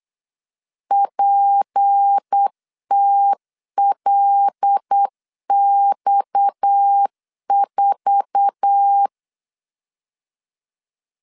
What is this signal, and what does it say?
Morse code, heard as a keyed tone, about 8 wpm, 800 Hz. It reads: PTLX4
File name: LEZIONI / 8 YJ45 / 8 YJ45 8wpm 3214